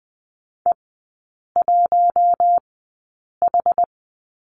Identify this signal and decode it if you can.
Morse code, heard as a keyed tone, about 20 wpm, spaced as for 8 wpm, 700 Hz. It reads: E1H
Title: 159 Lesson 26e - Letter g - Three Characters 20wpm